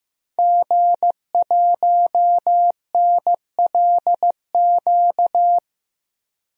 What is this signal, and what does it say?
Morse code, heard as a keyed tone, about 15 wpm, 700 Hz. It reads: G1NLQ
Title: DXCC-100 Call Signs - 5 Characters - 15wpm